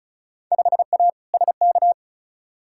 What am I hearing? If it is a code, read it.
Morse code, heard as a keyed tone, about 36 wpm, 700 Hz. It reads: 5A SK
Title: Field Day Exchanges Encore - 35wpm